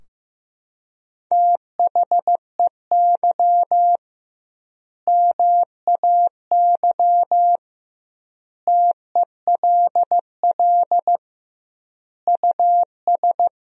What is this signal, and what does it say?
Morse code, heard as a keyed tone, about 15 wpm, 700 Hz. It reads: THEY MAY TELL US
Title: 2x Word-Spacing - Sentences from Top 300 Words - 15wpm